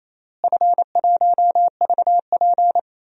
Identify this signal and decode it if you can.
Morse code, heard as a keyed tone, about 28 wpm, 700 Hz. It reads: F1VP
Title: DXCC-100 Call Signs - 4 Characters - Repeated 2x - 28wpm